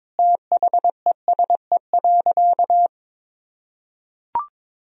Morse code, heard as a keyed tone, about 22 wpm, 700 Hz. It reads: THESE.
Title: Sentences from Top 1000 Words - Review Part 1 - 22wpm